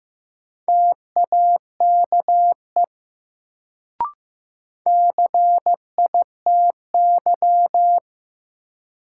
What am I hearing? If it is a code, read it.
Morse code, heard as a keyed tone, about 15 wpm, 700 Hz. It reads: TAKE CITY
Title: Sets of 2 Words - New Word Drill - 15wpm